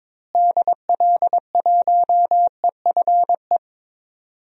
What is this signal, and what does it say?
Morse code, heard as a keyed tone, about 22 wpm, 700 Hz. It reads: DL1EFE
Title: DXCC-100 Call Signs - 6 Characters - Repeated 2x - 22wpm